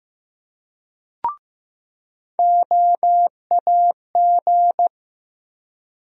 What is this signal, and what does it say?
Morse code, heard as a keyed tone, about 16 wpm, 700 Hz. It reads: OAG
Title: USA Call Sign Postfixes - 3 Letters 15wpm